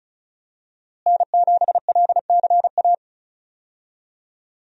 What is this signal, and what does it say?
Morse code, heard as a keyed tone, about 34 wpm, 700 Hz. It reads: N7LCA
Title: USA Call Signs - 5 Characters - Repeated 2x - 35wpm